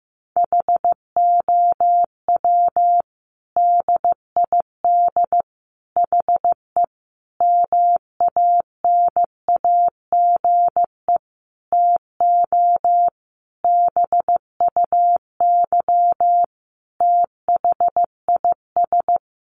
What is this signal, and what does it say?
Morse code, heard as a keyed tone, about 15 wpm, 700 Hz. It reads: HOW DID HE MANAGE TO BUY THIS
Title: Sentences from Top 1400 Words 15wpm